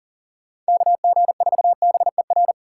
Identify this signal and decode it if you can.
Morse code, heard as a keyed tone, about 48 wpm, 700 Hz. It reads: KG4BER
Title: Speed Racing - USA Call Signs 40wpm